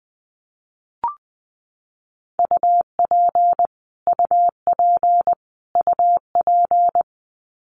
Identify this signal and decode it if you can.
Morse code, heard as a keyed tone, about 19 wpm, 700 Hz. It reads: UP UP UP
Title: Top 200 Words Repeated 20wpm